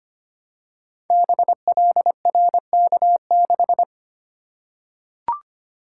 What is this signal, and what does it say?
Morse code, heard as a keyed tone, about 25 wpm, 700 Hz. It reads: BLRK6
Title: Code Groups - 5 Letter:Numbers - 25wpm